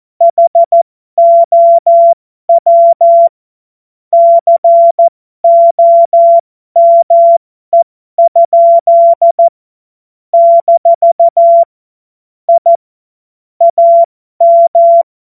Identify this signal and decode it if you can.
Morse code, heard as a keyed tone, about 14 wpm, spaced as for 12 wpm, 680 Hz. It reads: HOW COME? - I AM
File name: Morsekurs / Witze